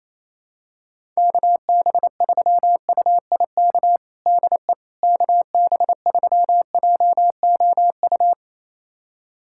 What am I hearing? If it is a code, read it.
Morse code, heard as a keyed tone, about 28 wpm, 700 Hz. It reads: KB3UIK DE KB3JOU
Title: USA Callsign DE Callsign - 6 Characters - 28wpm